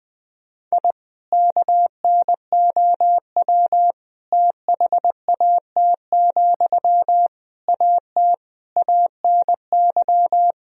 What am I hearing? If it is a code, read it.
Morse code, heard as a keyed tone, about 20 wpm, 700 Hz. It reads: I KNOW THAT, AT ANY
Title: Sentences from Top 1000 Words - Review Part 4 - 20wpm